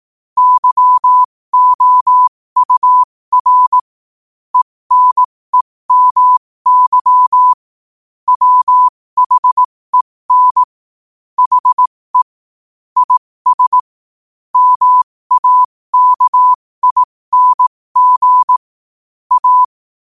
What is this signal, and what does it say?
Morse code, heard as a keyed tone, about 18 wpm, spaced as for 15 wpm, 1000 Hz. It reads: YOUR ENEMY WHEN HE IS MAKING A
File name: qotdfiles / 15wpm